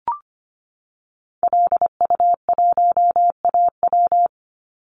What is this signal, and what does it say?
Morse code, heard as a keyed tone, about 25 wpm, 700 Hz. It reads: LU1AW
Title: Call Signs Repeated 2x - 25wpm